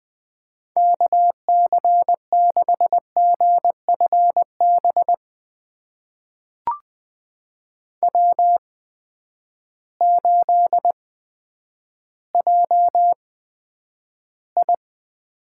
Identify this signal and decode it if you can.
Morse code, heard as a keyed tone, about 20 wpm, spaced as for 10 wpm, 700 Hz. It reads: KC6GFB W8JI
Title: USA Call Signs - 8x Character Spacing - 20wpm